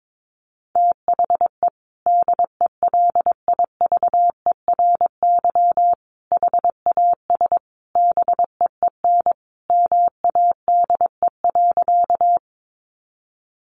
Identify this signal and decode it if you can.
Morse code, heard as a keyed tone, about 22 wpm, 700 Hz. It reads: THE DELIVERY HAS BEEN MADE.
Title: Sentences from Top 1500 Words 22wpm